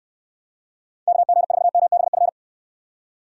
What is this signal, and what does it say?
Morse code, heard as a keyed tone, about 69 wpm, 700 Hz. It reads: KG4MBF